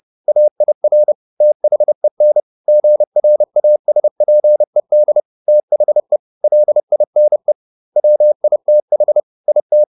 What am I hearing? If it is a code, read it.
Morse code, heard as a keyed tone, about 30 wpm, 600 Hz. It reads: AIR THEN GRASPED THE LINE WITH IT